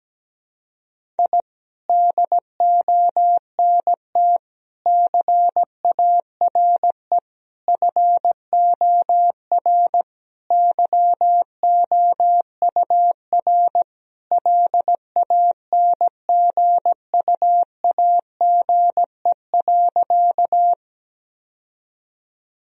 Morse code, heard as a keyed tone, about 17 wpm, 700 Hz. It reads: I DONT CARE FOR YOUR LANGUAGE.
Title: Sentences from Top 1000 Words - Review Part 3 - 17wpm